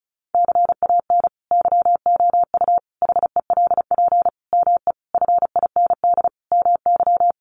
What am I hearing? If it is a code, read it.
Morse code, heard as a keyed tone, about 34 wpm, 700 Hz. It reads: CAN YOU HELP ME FIND MY
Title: Sentences from Top 200 Words 35wpm